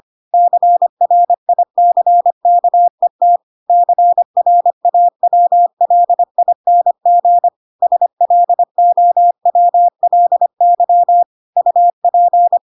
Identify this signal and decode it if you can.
Morse code, heard as a keyed tone, about 25 wpm, 700 Hz. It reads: CRICKET CRAWLING SLOWLY UP
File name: Pinocchio / Pinocchio ch4 25 25 ik1hge